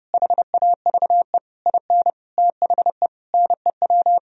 Morse code, heard as a keyed tone, about 30 wpm, 700 Hz. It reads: HAVE IN THE NEW
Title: Sentences from Top 1000 Words - Review Part 3 - 30wpm